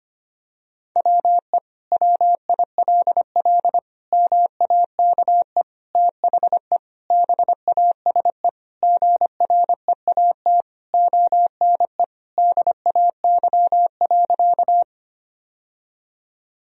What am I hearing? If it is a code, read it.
Morse code, heard as a keyed tone, about 25 wpm, 700 Hz. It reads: WE WILL MAKE THE BASE GREAT ONE DAY.
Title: Sentences from Top 1000 Words - Review Part 1 - 25wpm